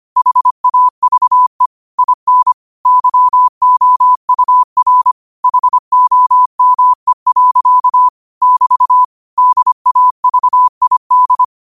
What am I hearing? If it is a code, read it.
Morse code, heard as a keyed tone, about 25 wpm, 1000 Hz. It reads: HAVE IN YOUR HOME. <BT> DAVID
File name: qotdfiles / 25wpm